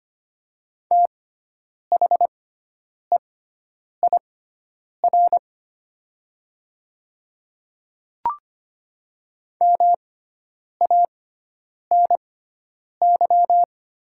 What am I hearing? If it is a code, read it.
Morse code, heard as a keyed tone, about 25 wpm, spaced as for 9 wpm, 700 Hz. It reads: THEIR MANY
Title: 6x Character Spacing - Top 100 Words - 25wpm